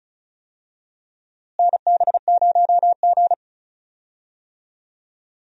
Morse code, heard as a keyed tone, about 34 wpm, 700 Hz. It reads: NB0G
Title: USA Call Signs - 4 Characters - Repeated 3x - 35wpm